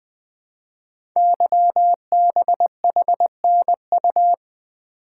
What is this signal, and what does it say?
Morse code, heard as a keyed tone, about 20 wpm, 700 Hz. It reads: YBHNU
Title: Code Groups - 5 Letter:Numbers - 20wpm